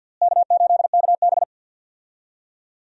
Morse code, heard as a keyed tone, about 53 wpm, 700 Hz. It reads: K8KB
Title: USA Call Signs - 4 Characters - Repeated 2x - 50wpm